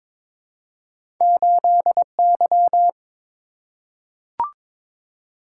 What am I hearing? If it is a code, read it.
Morse code, heard as a keyed tone, about 23 wpm, 700 Hz. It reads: 8Y
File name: practice / 22wpm / Two Letter-Number Combinations - 22wpm